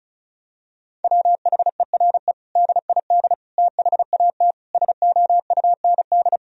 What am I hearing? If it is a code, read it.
Morse code, heard as a keyed tone, about 34 wpm, 700 Hz. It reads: WHERE DID THAT SOUND
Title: Sentences from Top 1000 Words - Review Part 1 - 35wpm